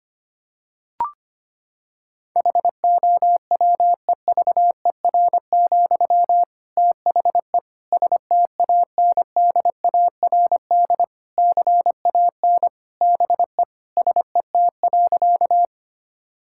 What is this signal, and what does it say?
Morse code, heard as a keyed tone, about 25 wpm, 700 Hz. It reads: HOWEVER, THE STANDARD CAN BE SET.
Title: Sentences from Top 1000 Words - Review Part 6 - 25wpm